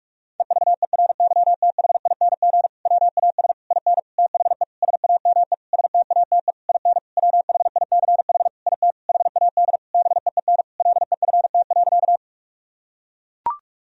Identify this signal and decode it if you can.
Morse code, heard as a keyed tone, about 41 wpm, 700 Hz. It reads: EVERYTHING WAS IN THE SAME STATE IN WHICH IT HAD BEEN LEFT.
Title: Sentences from Top 1000 Words - Review Part 2 - 45wpm